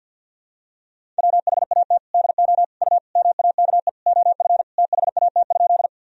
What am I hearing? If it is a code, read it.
Morse code, heard as a keyed tone, about 51 wpm, 700 Hz. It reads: WHAT DO U MAKE OF THAT?
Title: Sentences from Top 100 Words and Abbreviations 50wpm